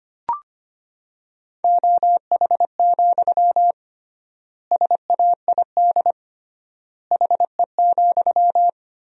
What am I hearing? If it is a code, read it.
Morse code, heard as a keyed tone, about 25 wpm, 700 Hz. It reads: OH, SAID HE,
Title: 3x Word-Spacing - Sentences from Top 500 Words - 25wpm